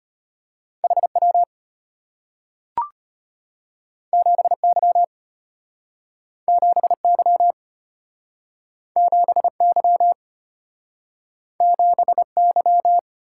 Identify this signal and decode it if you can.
Morse code, heard as a keyed tone, about 34 wpm, 700 Hz. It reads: HW 7Y 7Y 7Y 7Y